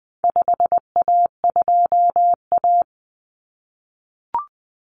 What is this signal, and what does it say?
Morse code, heard as a keyed tone, about 20 wpm, 700 Hz. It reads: 5A2A
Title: Common Cut Numbers - 4-Digit 20wpm